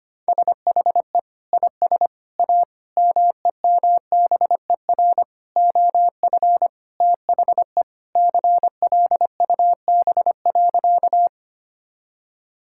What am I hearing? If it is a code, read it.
Morse code, heard as a keyed tone, about 25 wpm, 700 Hz. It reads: SHE IS A MEMBER OF THE CLUB.
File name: practice / 25wpm / Sentences with 3-Word Phrases - 25wpm